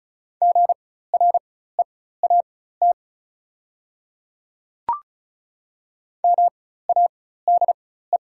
Morse code, heard as a keyed tone, about 34 wpm, spaced as for 16 wpm, 700 Hz. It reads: GREAT MADE
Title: 4x Character Spacing - Top 200 Words - 35wpm